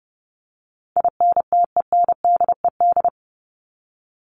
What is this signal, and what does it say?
Morse code, heard as a keyed tone, about 30 wpm, 700 Hz. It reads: INTENDED
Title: ED Suffix - Top 100 Words with Base Words 30wpm